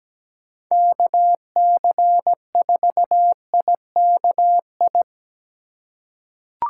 Morse code, heard as a keyed tone, about 17 wpm, 700 Hz. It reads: KC4IKI
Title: USA Call Signs - 8x Character Spacing - 17wpm